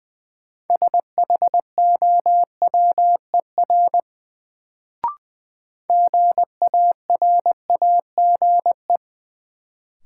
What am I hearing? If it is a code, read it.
Morse code, heard as a keyed tone, about 20 wpm, 700 Hz. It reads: SHOWER GARAGE
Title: Sentences from Top 1600 Words - New Word Drill Alt - 20wpm